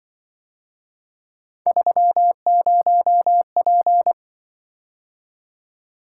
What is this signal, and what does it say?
Morse code, heard as a keyed tone, about 24 wpm, 700 Hz. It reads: 30P
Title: Speed-Racing - Three Letter-Number Combinations - 20wpm